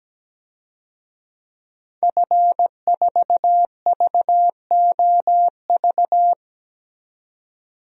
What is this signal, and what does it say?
Morse code, heard as a keyed tone, about 17 wpm, 700 Hz. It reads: F4VOV
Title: DXCC-100 Call Signs - 5 Characters - 17wpm